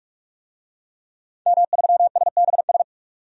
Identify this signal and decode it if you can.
Morse code, heard as a keyed tone, about 42 wpm, 700 Hz. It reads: M3SBS
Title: DXCC-100 Call Signs - 5 Characters - 45wpm